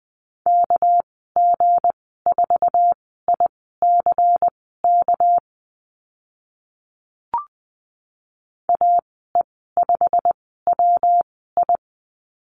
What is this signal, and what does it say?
Morse code, heard as a keyed tone, about 20 wpm, spaced as for 15 wpm, 700 Hz. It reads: KG4ICK AE5WI
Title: 213 Lesson 34g - Number 8 - Call Signs 20wpm